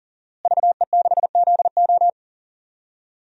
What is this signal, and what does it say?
Morse code, heard as a keyed tone, about 39 wpm, 700 Hz. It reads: VE6ZO
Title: DXCC-100 Call Signs - 5 Characters - Repeated 2x - 40wpm